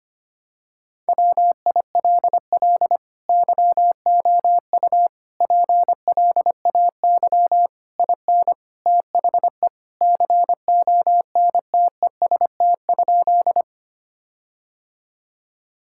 Morse code, heard as a keyed tone, about 25 wpm, 700 Hz. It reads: WILL YOU PLAY IN THE CONTEST?
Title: Sentences from Top 1600 Words 25wpm